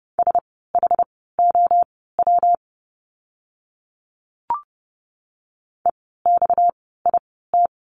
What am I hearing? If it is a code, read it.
Morse code, heard as a keyed tone, about 30 wpm, spaced as for 17 wpm, 700 Hz. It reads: SHOW EXIT
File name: course / 30:10wpm / 230 Lesson 37f - Letter x - Words 30wpm